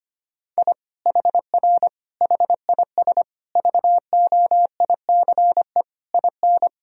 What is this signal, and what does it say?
Morse code, heard as a keyed tone, about 25 wpm, 700 Hz. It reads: I HR HIS VOICE IN
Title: Sentences from Top 500 Words and Abbreviations 25wpm